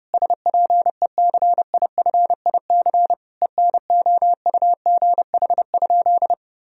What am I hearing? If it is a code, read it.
Morse code, heard as a keyed tone, about 30 wpm, 700 Hz. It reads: SPECIFIC ENOUGH?
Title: Sentences from Top 1000 Words - Review Part 3 - 30wpm